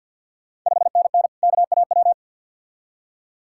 Morse code, heard as a keyed tone, about 51 wpm, 690 Hz. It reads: 5NN KAW
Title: Ontario QSO Party Exchange 50wpm